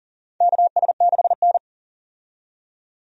Morse code, heard as a keyed tone, about 40 wpm, 700 Hz. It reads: KS6N
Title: USA Call Signs - 4 Characters - 40wpm